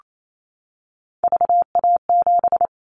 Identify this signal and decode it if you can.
Morse code, heard as a keyed tone, about 28 wpm, 700 Hz. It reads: VA7
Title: Callsign Prefixes for Top 50 DXCC - 28wpm